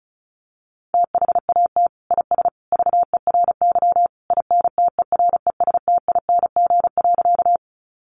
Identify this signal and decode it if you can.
Morse code, heard as a keyed tone, about 34 wpm, 700 Hz. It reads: THAT IS VERY INTERESTING.
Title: Sentences from Top 1000 Words 35wpm